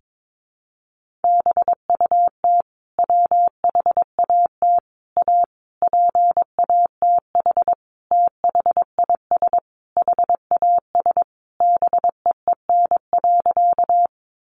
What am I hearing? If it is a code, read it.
Morse code, heard as a keyed tone, about 22 wpm, 700 Hz. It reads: BUT WHAT A PATH THIS HAS BEEN.
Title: Sentences from Top 1400 Words 22wpm